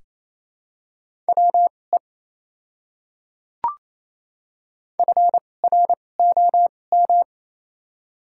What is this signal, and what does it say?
Morse code, heard as a keyed tone, about 28 wpm, spaced as for 20 wpm, 700 Hz. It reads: WE FROM
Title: 2x Character Spacing - Top 100 Words - 28wpm